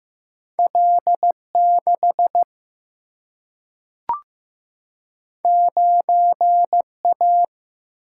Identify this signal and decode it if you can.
Morse code, heard as a keyed tone, about 15 wpm, 700 Hz. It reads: L 6 9 A
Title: Single-Letter Single-Number 15wpm